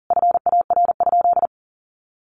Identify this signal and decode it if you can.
Morse code, heard as a keyed tone, about 37 wpm, 700 Hz. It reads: FAR?
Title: Sentences from Top 300 Words 40wpm